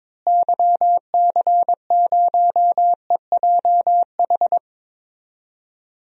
Morse code, heard as a keyed tone, about 22 wpm, 700 Hz. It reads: YC0EJH
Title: DXCC-100 Call Signs - 6 Characters - Repeated 3x - 22wpm